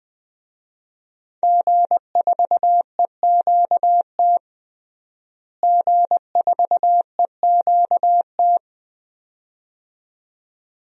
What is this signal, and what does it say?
Morse code, heard as a keyed tone, about 21 wpm, 700 Hz. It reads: G4EQT G4EQT
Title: DXCC-100 Call Signs - 5 Characters - Repeated 2x - 20wpm